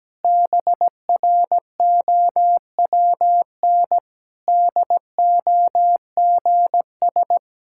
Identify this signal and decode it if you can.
Morse code, heard as a keyed tone, about 17 wpm, 700 Hz. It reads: BROWN DOGS
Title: Pangram Challenge v2 - 17wpm